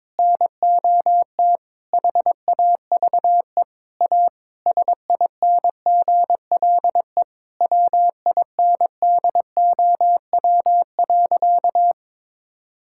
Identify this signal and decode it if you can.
Morse code, heard as a keyed tone, about 21 wpm, 700 Hz. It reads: NOT HAVE A SINGLE WINDOW.
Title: Sentences from Top 1600 Words 22wpm